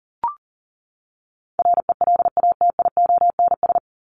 Morse code, heard as a keyed tone, about 40 wpm, 700 Hz. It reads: RELATIONS
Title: ES:S Suffix - Top 100 Words with Base Words 40wpm